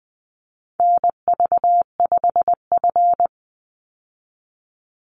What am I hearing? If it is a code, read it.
Morse code, heard as a keyed tone, about 20 wpm, 700 Hz. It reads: NV5F
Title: Call Signs - 20wpm